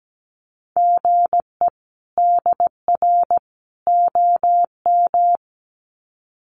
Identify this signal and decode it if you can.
Morse code, heard as a keyed tone, about 17 wpm, 700 Hz. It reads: GE DR OM
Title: Common QSO Phrases 17wpm